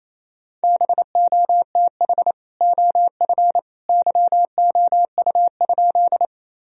Morse code, heard as a keyed tone, about 28 wpm, 700 Hz. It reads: BOTH OF YOU?
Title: Sentences from Top 1000 Words - Review Part 1 - 28wpm